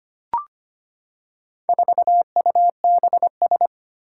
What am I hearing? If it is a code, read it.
Morse code, heard as a keyed tone, about 25 wpm, 700 Hz. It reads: 4UBS
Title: Four Challenging Letter-Number Combinations 25wpm